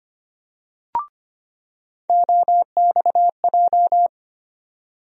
Morse code, heard as a keyed tone, about 25 wpm, 700 Hz. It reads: OXJ
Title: USA Call Sign Postfixes - 3 Letters 25wpm